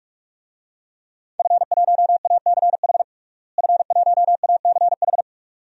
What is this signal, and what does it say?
Morse code, heard as a keyed tone, about 41 wpm, 700 Hz. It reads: F1ACH F1ACH
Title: DXCC-100 Call Signs - 5 Characters - Repeated 2x - 45wpm